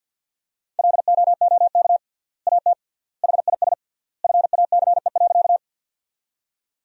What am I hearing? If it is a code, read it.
Morse code, heard as a keyed tone, about 50 wpm, 700 Hz. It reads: LOOK AT HIS FACE.
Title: 3x Word-Spacing - Sentences from Top 300 Words - 50wpm